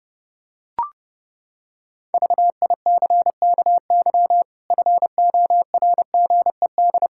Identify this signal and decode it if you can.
Morse code, heard as a keyed tone, about 28 wpm, 700 Hz. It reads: VICKY FORGED
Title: Pangram Challenge v2 - 30wpm